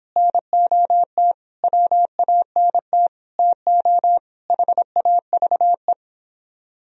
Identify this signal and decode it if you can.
Morse code, heard as a keyed tone, about 29 wpm, 700 Hz. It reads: NOT WANT TO HAVE